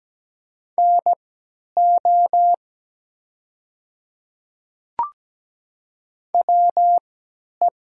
Morse code, heard as a keyed tone, about 17 wpm, spaced as for 10 wpm, 700 Hz. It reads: NO WE
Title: 3x Character Spacing - Top 100 Words - 17wpm